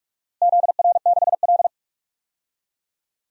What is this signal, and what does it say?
Morse code, heard as a keyed tone, about 47 wpm, 700 Hz. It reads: ZR6L
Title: DXCC-100 Call Signs - 4 Characters - Repeated 2x - 45wpm